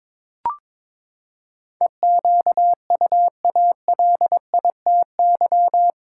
Morse code, heard as a keyed tone, about 22 wpm, 700 Hz. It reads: EQUALITY
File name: practice / 22wpm / Eight Letter Words - 22wpm